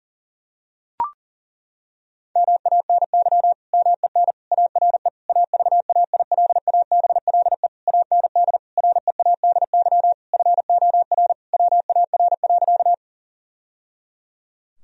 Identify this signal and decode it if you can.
Morse code, heard as a keyed tone, about 40 wpm, 700 Hz. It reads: MANY MEN ARE AVAILABLE AND READY FOR WAR.
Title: Sentences from Top 1000 Words - Review Part 3 - 40wpm